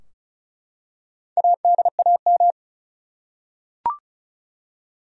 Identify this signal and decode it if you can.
Morse code, heard as a keyed tone, about 36 wpm, 700 Hz. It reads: ADAM
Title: USA Common Names - 35wpm